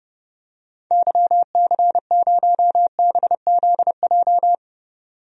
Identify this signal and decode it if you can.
Morse code, heard as a keyed tone, about 30 wpm, 700 Hz. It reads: YC0BZJ
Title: DXCC-100 Call Signs - 6 Characters - 30wpm